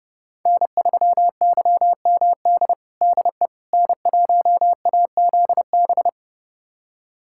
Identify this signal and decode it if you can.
Morse code, heard as a keyed tone, about 32 wpm, 700 Hz. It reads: N3YMD DE N1AZB
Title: USA Callsign DE Callsign - 5 Characters - 30wpm